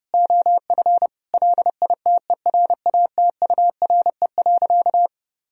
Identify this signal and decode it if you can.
Morse code, heard as a keyed tone, about 32 wpm, 700 Hz. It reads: OF LITERATURE.
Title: Sentences from Top 1000 Words - Review Part 3 - 30wpm